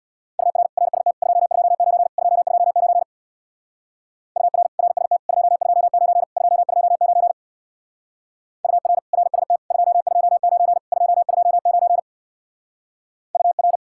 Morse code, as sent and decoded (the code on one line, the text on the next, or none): ..- .-. / .-. ... - / ..--- ..--- ----. / ..--- ..--- ----. / ..- .-. / .-. ... - / ..--- ..--- ----. / ..--- ..--- ----. / ..- .-. / .-. ... - / ..--- ..--- ----. / ..--- ..--- ----. / ..- .-.
UR RST 229 229 UR RST 229 229 UR RST 229 229 UR